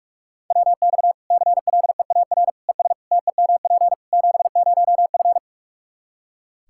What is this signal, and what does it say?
Morse code, heard as a keyed tone, about 46 wpm, 700 Hz. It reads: WX CLEAR ES TEMP 70F
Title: Practice QSO - Exchange 2 - Info - 45wpm